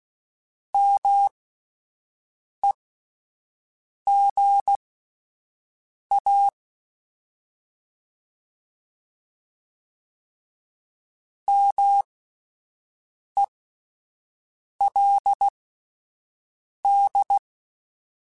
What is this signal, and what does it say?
Morse code, heard as a keyed tone, about 16 wpm, spaced as for 5 wpm, 800 Hz. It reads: MEGA MELD
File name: Alphabetical / Alphabetical mp3s / Ms @5